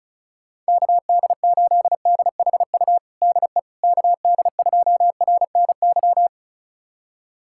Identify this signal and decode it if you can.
Morse code, heard as a keyed tone, about 36 wpm, 700 Hz. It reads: KD8DHU DE KD2RNY
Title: USA Callsign DE Callsign 35wpm